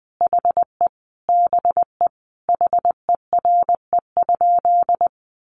Morse code, heard as a keyed tone, about 19 wpm, 700 Hz. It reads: HE BE HERE?
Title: Sentences from Top 400 Words 20wpm